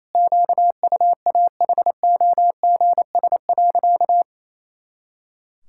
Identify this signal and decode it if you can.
Morse code, heard as a keyed tone, about 27 wpm, 700 Hz. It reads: QUAHOGS.
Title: Pangram Challenge v2 - 28wpm